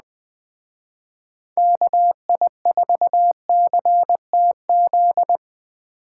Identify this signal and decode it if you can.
Morse code, heard as a keyed tone, about 20 wpm, 700 Hz. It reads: KI4CTZ
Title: Speed Racing - USA 2x3 Call Signs 17wpm